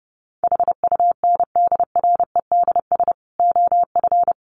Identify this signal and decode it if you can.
Morse code, heard as a keyed tone, about 30 wpm, 700 Hz. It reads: HUNDREDS OF
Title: Sets of 4 Words 30wpm